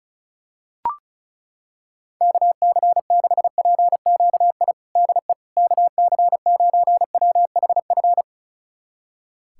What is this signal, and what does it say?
Morse code, heard as a keyed tone, about 34 wpm, 700 Hz. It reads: KC6PQI DE KC9WHF
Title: USA Callsign DE Callsign - 6 Characters - 35wpm